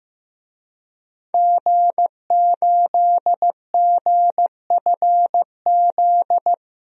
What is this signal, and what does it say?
Morse code, heard as a keyed tone, about 15 wpm, 700 Hz. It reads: G8GFZ
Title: DXCC-100 Call Signs - 5 Characters - 15wpm